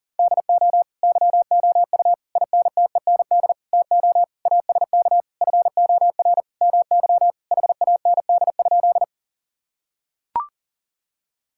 Morse code, heard as a keyed tone, about 40 wpm, 700 Hz. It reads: DO YOU INTEND TO ASK FOR MY HAND?
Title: Sentences from Top 1600 Words 40wpm